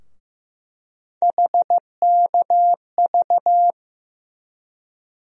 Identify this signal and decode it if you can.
Morse code, heard as a keyed tone, about 15 wpm, 700 Hz. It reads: HKV
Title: USA Call Sign Postfixes - 3 Letters 15wpm